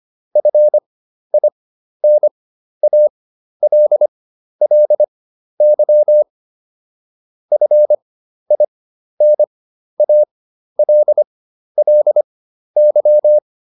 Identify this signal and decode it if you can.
Morse code, heard as a keyed tone, about 25 wpm, spaced as for 12 wpm, 600 Hz. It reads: FINALLY FINALLY